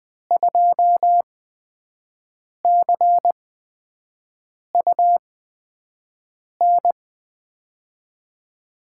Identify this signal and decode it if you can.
Morse code, heard as a keyed tone, about 20 wpm, spaced as for 5 wpm, 700 Hz. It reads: 2CUN
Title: USA Call Signs - 8x Character Spacing - 20wpm